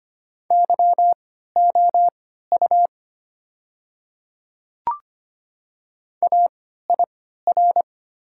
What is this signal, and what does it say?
Morse code, heard as a keyed tone, about 25 wpm, spaced as for 14 wpm, 700 Hz. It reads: YOU AIR
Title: 3x Character Spacing - Top 200 Words - 25wpm